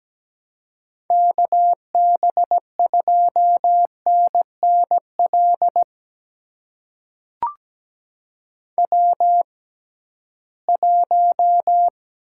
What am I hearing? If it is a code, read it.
Morse code, heard as a keyed tone, about 17 wpm, spaced as for 13 wpm, 700 Hz. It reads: KB2NNL W1
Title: USA Call Signs - 6x Character Spacing - 17wpm